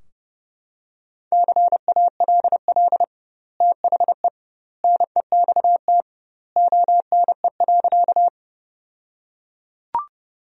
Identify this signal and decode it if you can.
Morse code, heard as a keyed tone, about 32 wpm, 700 Hz. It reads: CALL THE NEXT ONE.
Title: 2x Word-Spacing - Sentences from Top 300 Words - 30wpm